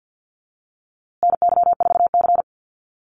Morse code, heard as a keyed tone, about 49 wpm, 700 Hz. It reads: NY4C
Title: USA Call Signs Repeated 3x 50wpm